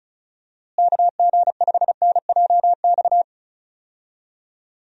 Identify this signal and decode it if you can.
Morse code, heard as a keyed tone, about 36 wpm, 700 Hz. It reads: KG5NJX
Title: USA Call Signs - 6 Characters - Repeated 2x - 35wpm